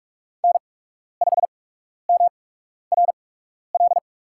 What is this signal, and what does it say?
Morse code, heard as a keyed tone, about 44 wpm, spaced as for 12 wpm, 700 Hz. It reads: N5MRL
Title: USA Call Signs - 5 Characters - 8x Character Spacing - 45wpm